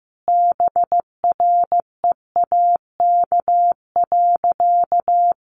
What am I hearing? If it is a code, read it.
Morse code, heard as a keyed tone, about 15 wpm, 700 Hz. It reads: BREAK.
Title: Sentences from Top 1300 Words 15wpm